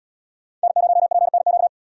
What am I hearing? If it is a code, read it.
Morse code, heard as a keyed tone, about 74 wpm, 690 Hz. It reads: N0PNQ